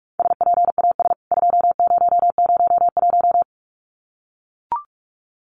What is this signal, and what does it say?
Morse code, heard as a keyed tone, about 44 wpm, 700 Hz. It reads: SPAS 2001
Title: CWT Exchange 45wpm